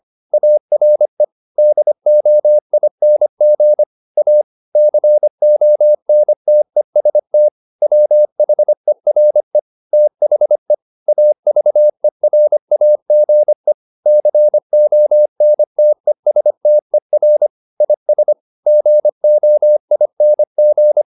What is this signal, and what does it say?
Morse code, heard as a keyed tone, about 25 wpm, 600 Hz. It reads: ARE DOING A CONTEST WHERE THE AVERAGE CONTESTER IS GOING